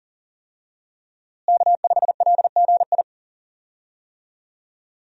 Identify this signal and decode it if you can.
Morse code, heard as a keyed tone, about 40 wpm, 700 Hz. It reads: K5LGI
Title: USA Call Signs - 5 Characters - Repeated 3x - 40wpm